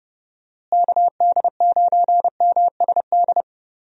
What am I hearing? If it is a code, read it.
Morse code, heard as a keyed tone, about 30 wpm, 700 Hz. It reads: KD9MSD
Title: USA Call Signs - 6 Characters - Repeated 2x - 30wpm